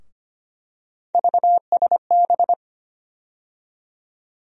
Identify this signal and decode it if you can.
Morse code, heard as a keyed tone, about 25 wpm, 700 Hz. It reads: VSB
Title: USA Call Sign Postfixes - 3 Letters 25wpm